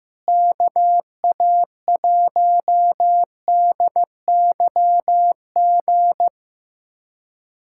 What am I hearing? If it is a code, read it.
Morse code, heard as a keyed tone, about 15 wpm, 700 Hz. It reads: KA1DYG
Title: USA Callsign DE Callsign - 6 Characters - 15wpm